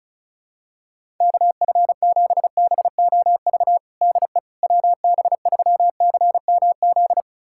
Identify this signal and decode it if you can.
Morse code, heard as a keyed tone, about 36 wpm, 700 Hz. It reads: KF7BOV DE WB3CMZ
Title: USA Callsign DE Callsign 35wpm